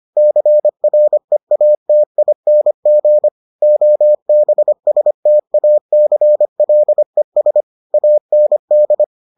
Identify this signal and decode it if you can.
Morse code, heard as a keyed tone, about 25 wpm, 600 Hz. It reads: CREATING OBSTACLES AND